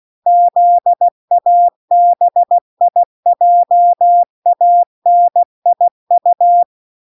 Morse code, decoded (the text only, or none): ZABIJANIU